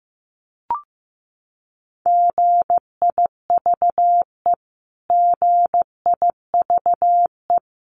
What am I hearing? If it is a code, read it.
Morse code, heard as a keyed tone, about 15 wpm, 700 Hz. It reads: GIVE GIVE
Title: Top 300 Words Repeated 15wpm